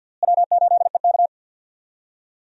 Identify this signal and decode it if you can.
Morse code, heard as a keyed tone, about 52 wpm, 700 Hz. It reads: W8EK
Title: USA Call Signs - 4 Characters - Repeated 2x - 50wpm